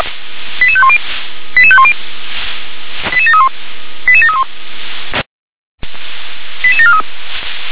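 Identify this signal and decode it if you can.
Five-tone selective call (ZVEI), heard as quick runs of five tones, tones of about 70 ms. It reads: 88510 88510 88731 88731 88743
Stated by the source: Five-tone ZVEI selective call received by SDR.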